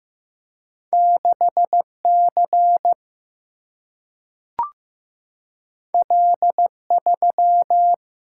Morse code, heard as a keyed tone, about 15 wpm, 700 Hz. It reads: 6 C L 3
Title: Single-Letter Single-Number 15wpm